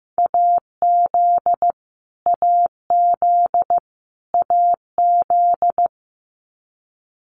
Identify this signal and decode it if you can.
Morse code, heard as a keyed tone, about 15 wpm, 700 Hz. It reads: AZ AZ AZ
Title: US States Repeated 3x 15wpm